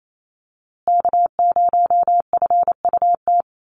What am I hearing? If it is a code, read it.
Morse code, heard as a keyed tone, about 28 wpm, 700 Hz. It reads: K0FUT
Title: USA Call Signs Repeated 3x 28wpm